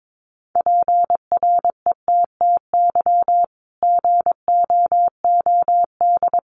Morse code, heard as a keyed tone, about 22 wpm, 700 Hz. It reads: PRETTY GOOD
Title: Sentences from Top 1400 Words 22wpm